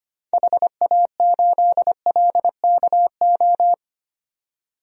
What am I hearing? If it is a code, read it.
Morse code, heard as a keyed tone, about 25 wpm, 700 Hz. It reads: HA8LKO
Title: DXCC-100 Call Signs - 6 Characters - 25wpm